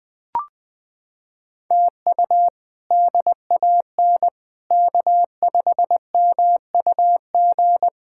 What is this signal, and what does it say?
Morse code, heard as a keyed tone, about 20 wpm, 700 Hz. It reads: TU DAN K5MUG
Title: K1USN SST QSO Practice - 20wpm